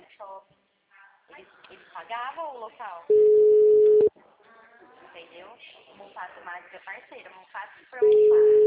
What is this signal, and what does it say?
Ringback tone (Europe, 425 Hz), with speech.